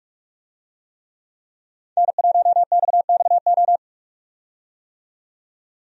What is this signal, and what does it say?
Morse code, heard as a keyed tone, about 43 wpm, 700 Hz. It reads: N1XXO
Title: USA Call Signs - 5 Characters - 45wpm